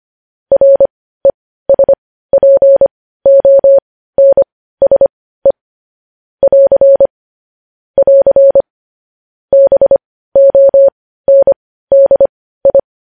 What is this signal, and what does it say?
Morse code, heard as a keyed tone, about 25 wpm, spaced as for 15 wpm, 550 Hz. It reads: RESPONSE <AR> <AR> BONDI
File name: podcast / News headlines-1x-25x15